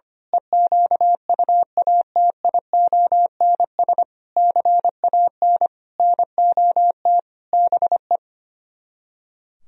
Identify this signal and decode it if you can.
Morse code, heard as a keyed tone, about 24 wpm, 700 Hz. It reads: EQUATIONS CAN NOT BE
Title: Speed-Racing - Sets of 5 Words 25wpm